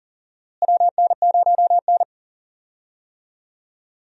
Morse code, heard as a keyed tone, about 40 wpm, 700 Hz. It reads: WN0N
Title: USA Call Signs - 4 Characters - Repeated 2x - 40wpm